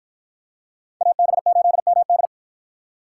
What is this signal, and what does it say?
Morse code, heard as a keyed tone, about 50 wpm, 700 Hz. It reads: AB8MD